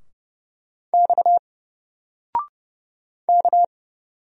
Morse code, heard as a keyed tone, about 30 wpm, 700 Hz. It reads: X K
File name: Single-Letter-Number-30wpm